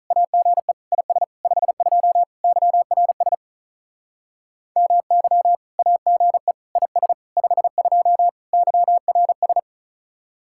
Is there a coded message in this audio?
Yes: Morse code from 0.1 to 9.6 s, heard as a keyed tone, about 46 wpm, 700 Hz. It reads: AGE IS 52 YRS MY AGE IS 52 YRS